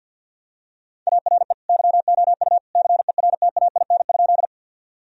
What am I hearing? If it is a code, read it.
Morse code, heard as a keyed tone, about 50 wpm, 700 Hz. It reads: ARE YOU CERTAIN?
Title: Sentences from Top 1000 Words - Review Part 2 - 50wpm